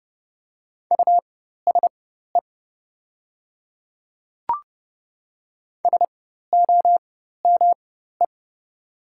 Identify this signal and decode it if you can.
Morse code, heard as a keyed tone, about 30 wpm, spaced as for 14 wpm, 700 Hz. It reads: USE SOME
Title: 4x Character Spacing - Top 200 Words - 30wpm